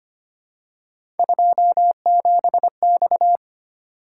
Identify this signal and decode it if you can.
Morse code, heard as a keyed tone, about 25 wpm, 700 Hz. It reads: 27X
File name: Three Letter-Number Combinations - 25wpm